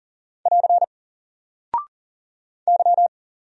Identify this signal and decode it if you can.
Morse code, heard as a keyed tone, about 40 wpm, 700 Hz. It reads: <AR> Y
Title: Single Character (Letter:Number:Punctuation:Prosign) 40wpm